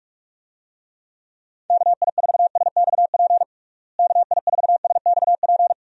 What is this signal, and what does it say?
Morse code, heard as a keyed tone, about 46 wpm, 700 Hz. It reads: KI4SXP KI4SXP
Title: USA Call Signs - 6 Characters - Repeated 3x - 45wpm